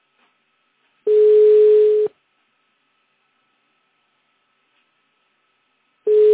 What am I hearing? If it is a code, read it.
Ringback tone (Europe, 425 Hz).
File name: 1015. .5f016bfd-150d-4b42-bfe3